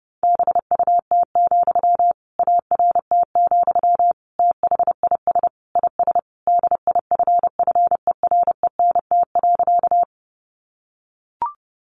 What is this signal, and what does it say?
Morse code, heard as a keyed tone, about 30 wpm, 700 Hz. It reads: BUT, ART, THIS IS DIFFERENT.
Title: Sentences from Top 600 Words 30wpm